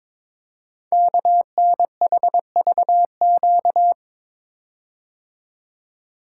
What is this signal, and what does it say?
Morse code, heard as a keyed tone, about 22 wpm, 700 Hz. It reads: KNHVQ
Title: Code Groups - 5 Letters - 22wpm